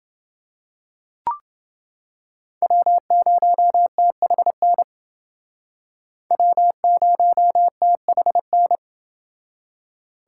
Morse code, heard as a keyed tone, about 28 wpm, 700 Hz. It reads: W0THN W0THN